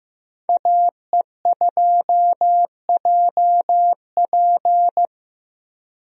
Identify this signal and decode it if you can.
Morse code, heard as a keyed tone, about 15 wpm, 700 Hz. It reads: AE2JP
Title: USA Call Signs - 5 Characters - Repeated 3x - 15wpm